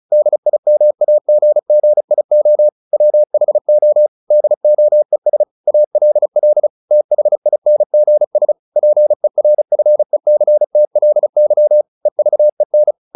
Morse code, heard as a keyed tone, about 34 wpm, 600 Hz. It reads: DIMAGGIO WHO DOES ALL THINGS PERFECTLY EVEN
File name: the old man and the sea / the old man and the sea 1 35wpm